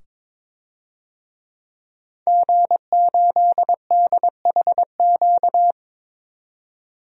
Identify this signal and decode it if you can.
Morse code, heard as a keyed tone, about 21 wpm, 700 Hz. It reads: G8DHQ
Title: DXCC-100 Call Signs - 5 Characters - 22wpm